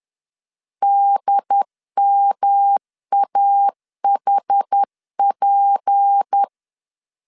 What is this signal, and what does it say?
Morse code, heard as a keyed tone, about 10 wpm, 800 Hz. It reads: DMAHP